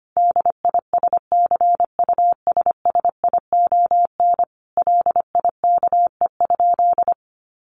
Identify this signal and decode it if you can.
Morse code, heard as a keyed tone, about 25 wpm, 700 Hz. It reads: DISCUSSION LIKE?
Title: Sentences from Top 1500 Words 25wpm